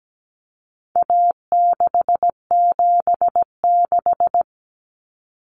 Morse code, heard as a keyed tone, about 17 wpm, 700 Hz. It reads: A676
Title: Common Cut Numbers - 4-Digit 17wpm